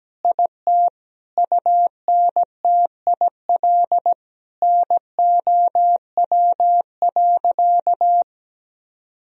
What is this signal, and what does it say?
Morse code, heard as a keyed tone, about 17 wpm, 700 Hz. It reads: IT UNTIL NOW.
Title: Sentences from Top 1000 Words - Review Part 1 - 17wpm